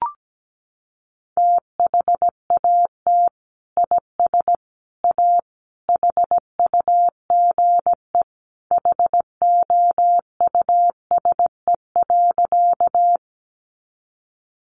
Morse code, heard as a keyed tone, about 17 wpm, 700 Hz. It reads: THAT IS A HUGE HOUSE.
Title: Sentences from Top 1100 Words 17wpm